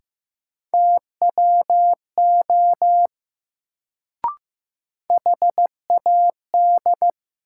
Morse code, heard as a keyed tone, about 15 wpm, 700 Hz. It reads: TWO HAD
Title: Sets of 2 Words - New Word Drill - 15wpm